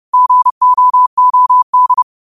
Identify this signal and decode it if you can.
Morse code, heard as a keyed tone, about 30 wpm, 1000 Hz. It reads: GOOD